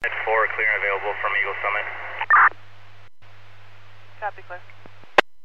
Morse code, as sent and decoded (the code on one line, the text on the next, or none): none